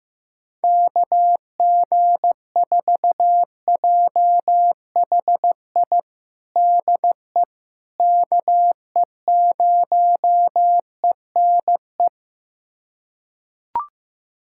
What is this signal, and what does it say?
Morse code, heard as a keyed tone, about 16 wpm, 700 Hz. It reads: KG4JHI DE KE0ENE
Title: USA Callsign DE Callsign 15wpm